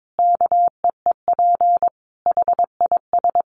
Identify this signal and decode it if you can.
Morse code, heard as a keyed tone, about 22 wpm, 700 Hz. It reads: KEEP HIS
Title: Sentences from Top 1500 Words 22wpm